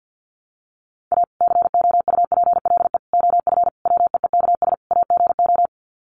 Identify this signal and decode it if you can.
Morse code, heard as a keyed tone, about 50 wpm, 700 Hz. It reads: A COUPLE OF WEEKS AGO
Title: Sets of 5 Words 50wpm